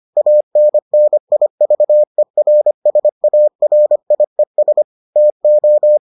Morse code, heard as a keyed tone, about 25 wpm, 600 Hz. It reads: ANNIVERSARIES TO